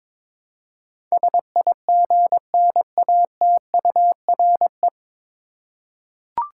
Morse code, heard as a keyed tone, about 22 wpm, 700 Hz. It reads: SIGNATURE
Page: Morse Code Ninja practice 22wpm